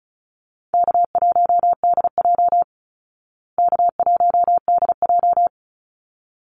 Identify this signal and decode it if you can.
Morse code, heard as a keyed tone, about 34 wpm, 700 Hz. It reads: K1DJ K1DJ
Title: Call Signs Repeated 3x - 35wpm